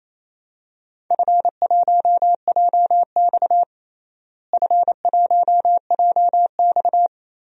Morse code, heard as a keyed tone, about 28 wpm, 700 Hz. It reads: F1JX F1JX
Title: DXCC-100 Call Signs - 4 Characters - Repeated 3x - 28wpm